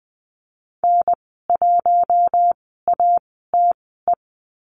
Morse code, heard as a keyed tone, about 20 wpm, spaced as for 14 wpm, 700 Hz. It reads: N1ATE
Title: 048 Lesson 9g - Number 4 - Call Signs 20wpm